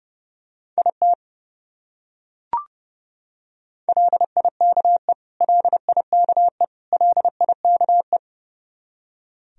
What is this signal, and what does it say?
Morse code, heard as a keyed tone, about 29 wpm, 700 Hz. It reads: IT LIKE LIKE LIKE
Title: Top 100 Words Repeated 30wpm